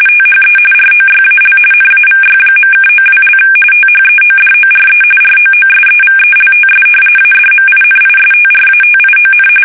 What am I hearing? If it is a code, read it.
RTTY, about 45.45 baud. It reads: THE ROYAL NAVY.(92) THEODORE ROOSEVELTS ADMINISTRATION
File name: fsk850 / usnavy3